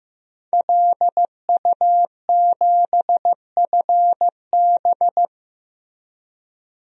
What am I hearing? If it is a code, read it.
Morse code, heard as a keyed tone, about 15 wpm, 700 Hz. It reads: LU7FB
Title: DXCC-100 Call Signs - 5 Characters - Repeated 3x - 15wpm